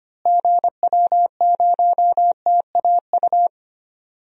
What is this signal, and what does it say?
Morse code, heard as a keyed tone, about 25 wpm, 700 Hz. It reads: GW0TAU
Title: DXCC-100 Call Signs - 6 Characters - Repeated 3x - 25wpm